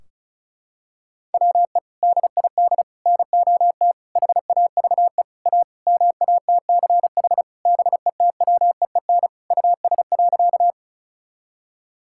Morse code, heard as a keyed tone, about 36 wpm, 700 Hz. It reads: WE DID NOT HAVE A MATCH BETWEEN US.
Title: Sentences from Top 1600 Words 35wpm